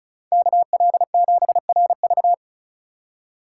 Morse code, heard as a keyed tone, about 36 wpm, 700 Hz. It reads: KL7RV
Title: USA Call Signs - 4x Character Spacing - 35wpm